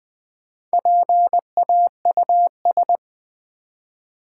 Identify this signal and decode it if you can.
Morse code, heard as a keyed tone, about 20 wpm, 700 Hz. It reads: PAUS
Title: Build A Word - 1 Letter At A Time - 5-Letter Words - 20wpm